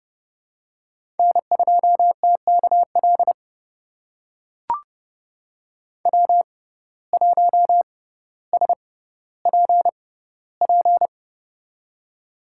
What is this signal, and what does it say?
Morse code, heard as a keyed tone, about 30 wpm, spaced as for 15 wpm, 700 Hz. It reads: N2TKL W1SPP
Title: USA Call Signs - 5 Characters - 6x Character Spacing - 30wpm